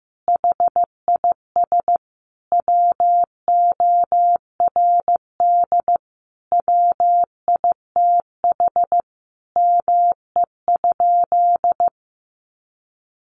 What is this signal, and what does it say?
Morse code, heard as a keyed tone, about 15 wpm, 700 Hz. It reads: HIS WORD WITH ME?
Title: Sentences from Top 300 Words 15wpm